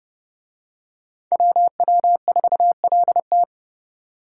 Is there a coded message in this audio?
Yes: Morse code from 1.3 to 3.4 s, heard as a keyed tone, about 29 wpm, 700 Hz. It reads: WW4LT